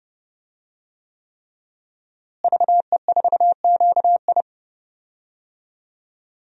Morse code, heard as a keyed tone, about 30 wpm, 700 Hz. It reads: VE4QI
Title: DXCC-100 Call Signs - 5 Characters - Repeated 3x - 30wpm